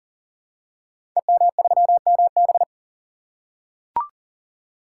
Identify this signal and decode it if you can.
Morse code, heard as a keyed tone, about 39 wpm, 690 Hz. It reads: EM3MB
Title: Code Groups - 5 Letter:Numbers - 40wpm